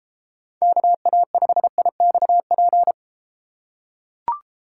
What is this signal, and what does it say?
Morse code, heard as a keyed tone, about 33 wpm, 700 Hz. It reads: KA5IXP